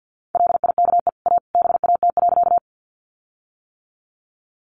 Morse code, heard as a keyed tone, about 50 wpm, 700 Hz. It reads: LIKE A BAT.
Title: Sentences from Top 1500 Words 50wpm